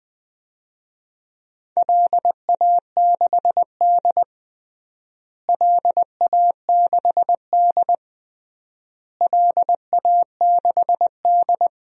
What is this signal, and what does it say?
Morse code, heard as a keyed tone, about 20 wpm, 700 Hz. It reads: LA6D LA6D LA6D
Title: DXCC-100 Call Signs - 4 Characters - Repeated 3x - 20wpm